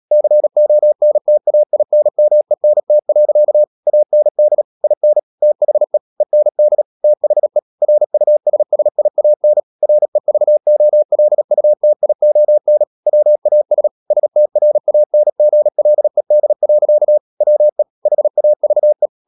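Morse code, heard as a keyed tone, about 37 wpm, 600 Hz. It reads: CONTAINMENT. AND IN THE END THE RUSSIAN REVOLUTION WAS STRANGLED. WE HAVE
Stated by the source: Morse code practice at 35–39 wpm sending English text of Ken Follett's novel Never, from CW med Gnister.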